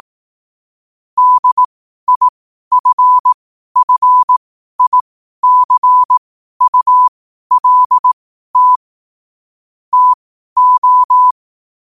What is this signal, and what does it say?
Morse code, heard as a keyed tone, about 18 wpm, spaced as for 12 wpm, 1000 Hz. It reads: DIFFICULT TO
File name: qotdfiles / 12wpm